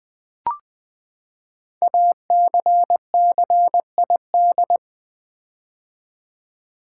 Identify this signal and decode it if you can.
Morse code, heard as a keyed tone, about 20 wpm, 700 Hz. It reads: ACCID
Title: Build A Word - 5 Letters At A Time - 12-Letter Words - 20wpm